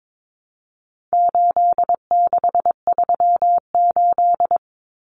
Morse code, heard as a keyed tone, about 22 wpm, 700 Hz. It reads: 8638
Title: Common Cut Numbers - 4-Digit 22wpm